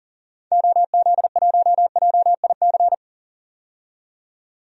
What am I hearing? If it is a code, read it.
Morse code, heard as a keyed tone, about 40 wpm, 700 Hz. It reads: OZ1JIC
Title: DXCC-100 Call Signs - 6 Characters - 40wpm